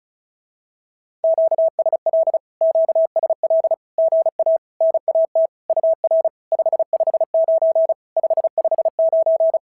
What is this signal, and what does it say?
Morse code, heard as a keyed tone, about 33 wpm, 650 Hz. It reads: QSL QSL GA NAT UR 559 559
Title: SOTA QSO Practice 35wpm